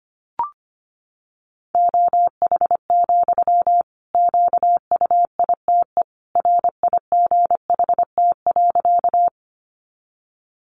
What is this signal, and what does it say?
Morse code, heard as a keyed tone, about 25 wpm, 700 Hz. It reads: OH, QUITE RIGHT.
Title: Sentences from Top 700 Words 25wpm